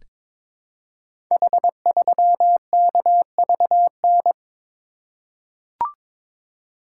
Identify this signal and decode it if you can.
Morse code, heard as a keyed tone, about 22 wpm, 700 Hz. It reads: H3KVN
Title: Code Groups - 5 Letter:Numbers - 22wpm